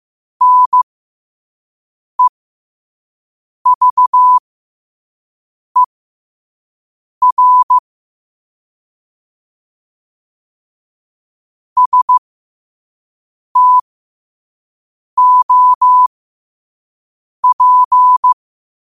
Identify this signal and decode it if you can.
Morse code, heard as a keyed tone, about 15 wpm, spaced as for 5 wpm, 1000 Hz. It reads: NEVER STOP
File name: qotdfiles / 5wpm